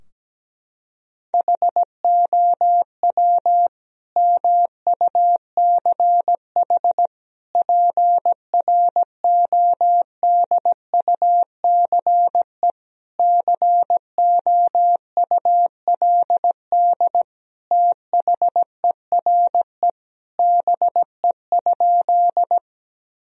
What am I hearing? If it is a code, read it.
Morse code, heard as a keyed tone, about 17 wpm, 700 Hz. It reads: HOW MUCH PRODUCE COULD THERE BE?
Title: Sentences from Top 1000 Words - Review Part 3 - 17wpm